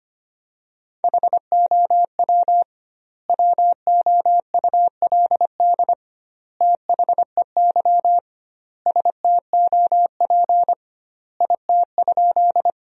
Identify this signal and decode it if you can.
Morse code, heard as a keyed tone, about 25 wpm, 700 Hz. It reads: HOW WOULD THEY STOP IT?
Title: 2x Word-Spacing - Sentences from Top 300 Words - 25wpm